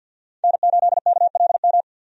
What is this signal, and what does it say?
Morse code, heard as a keyed tone, about 50 wpm, 700 Hz. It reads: N8KLM
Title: USA Call Signs - 5 Characters - 50wpm